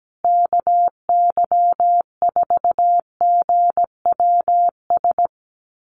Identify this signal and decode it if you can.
Morse code, heard as a keyed tone, about 17 wpm, 700 Hz. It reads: KY4GWS
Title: USA Call Signs Repeated 3x 17wpm